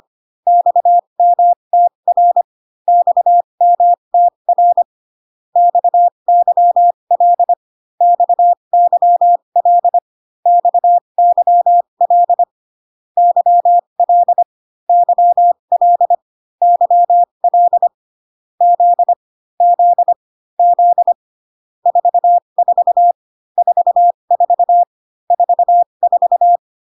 Morse code, heard as a keyed tone, about 25 wpm, spaced as for 22 wpm, 700 Hz. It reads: XMTR XMTR XYL XYL XYL YL YL YL Z Z Z 44 44 44
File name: CW common abbreviations alphabetical repeated 3 times 25 18wpm